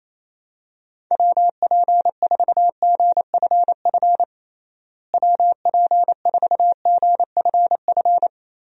Morse code, heard as a keyed tone, about 28 wpm, 700 Hz. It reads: WP4GFF WP4GFF
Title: USA Call Signs - 6 Characters - Repeated 3x - 28wpm